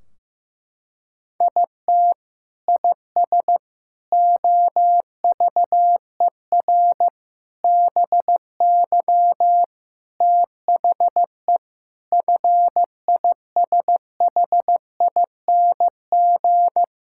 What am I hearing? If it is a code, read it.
Morse code, heard as a keyed tone, about 15 wpm, 700 Hz. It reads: IT IS OVER BY THE FISHING
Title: Sentences from Top 1600 Words 15wpm